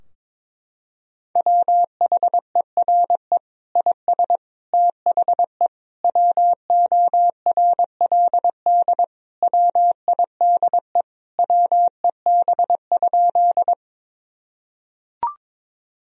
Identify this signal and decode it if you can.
Morse code, heard as a keyed tone, about 22 wpm, 700 Hz. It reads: WHERE IS THE WORLD WIDE WEB?
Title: Sentences from Top 1000 Words - Review Part 4 - 22wpm